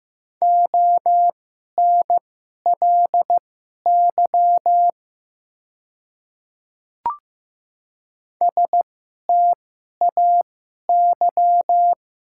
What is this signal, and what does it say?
Morse code, heard as a keyed tone, about 15 wpm, spaced as for 11 wpm, 700 Hz. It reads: ONLY STAY
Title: 2x Character Spacing - Top 500 Words - 15wpm